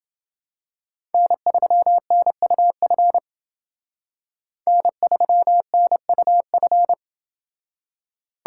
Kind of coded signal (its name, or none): Morse code